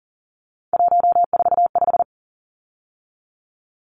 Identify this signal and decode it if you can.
Morse code, heard as a keyed tone, about 40 wpm, 700 Hz. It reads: 145
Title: Numbers - 3-Digit 40wpm